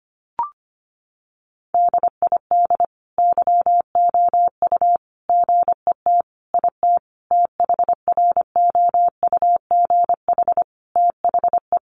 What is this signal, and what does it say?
Morse code, heard as a keyed tone, about 25 wpm, 700 Hz. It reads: DID YOU GET IT THROUGH THE
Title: Sentences from Top 1300 Words 25wpm